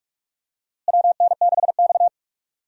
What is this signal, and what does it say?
Morse code, heard as a keyed tone, about 46 wpm, 700 Hz. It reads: WN6X
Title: USA Call Signs - 4 Characters - Repeated 3x - 45wpm